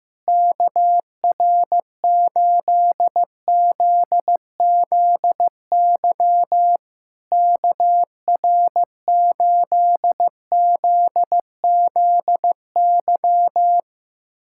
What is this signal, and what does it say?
Morse code, heard as a keyed tone, about 15 wpm, 700 Hz. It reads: KR8ZZY KR8ZZY
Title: SOTA QSO Practice 15wpm